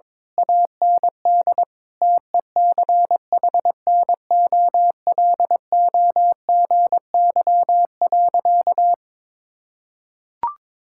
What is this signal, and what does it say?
Morse code, heard as a keyed tone, about 22 wpm, 700 Hz. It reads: AND TECHNOLOGY.
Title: Sentences from Top 1000 Words - Review Part 6 - 22wpm